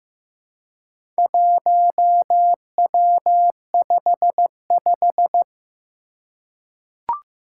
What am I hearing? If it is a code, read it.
Morse code, heard as a keyed tone, about 15 wpm, 700 Hz. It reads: 1W55
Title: Four Challenging Letter-Number Combinations 15wpm